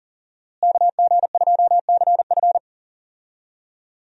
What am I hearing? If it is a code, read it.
Morse code, heard as a keyed tone, about 37 wpm, 700 Hz. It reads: KG2CF
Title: USA Call Signs - 5 Characters - Repeated 3x - 40wpm